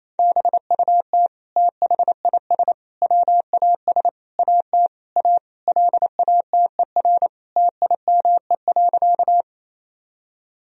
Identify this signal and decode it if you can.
Morse code, heard as a keyed tone, about 28 wpm, 700 Hz. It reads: BUT THIS WAS AT A LATER TIME.
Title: Sentences from Top 1000 Words - Review Part 4 - 28wpm